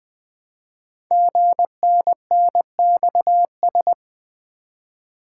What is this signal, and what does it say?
Morse code, heard as a keyed tone, about 20 wpm, 700 Hz. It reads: GNNXS
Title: Code Groups - 5 Letters - 20wpm